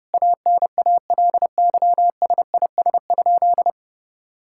Morse code, heard as a keyed tone, about 32 wpm, 700 Hz. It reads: ANALYSIS?
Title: Sentences from Top 1000 Words - Review Part 5 - 30wpm